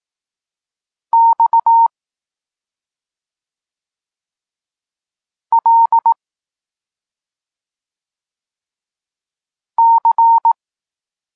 Morse code, heard as a keyed tone, about 18 wpm, 920 Hz. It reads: X L C
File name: Tekst-34 4